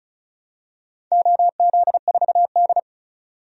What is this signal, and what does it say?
Morse code, heard as a keyed tone, about 36 wpm, 700 Hz. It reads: OZ4D
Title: DXCC-100 Call Signs - 4 Characters - 35wpm